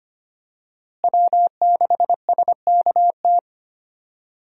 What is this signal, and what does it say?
Morse code, heard as a keyed tone, about 25 wpm, 700 Hz. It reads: W6SKT
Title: USA Call Signs - 5 Characters - Repeated 2x - 25wpm